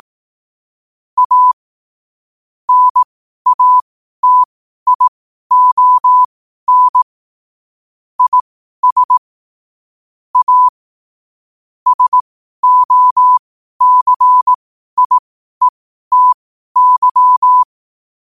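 Morse code, heard as a keyed tone, about 18 wpm, spaced as for 12 wpm, 1000 Hz. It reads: A NATION IS A SOCIETY
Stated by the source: Morse code quote of the day at 12 wpm.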